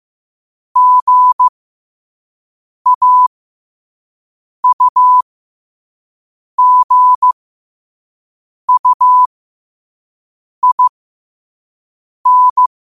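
Morse code, heard as a keyed tone, about 15 wpm, spaced as for 5 wpm, 1000 Hz. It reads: GAUGUIN
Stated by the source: Morse code sent at 5 wpm.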